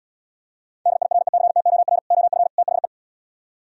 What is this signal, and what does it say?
Morse code, heard as a keyed tone, about 76 wpm, 700 Hz. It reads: DIRECTOR OF THE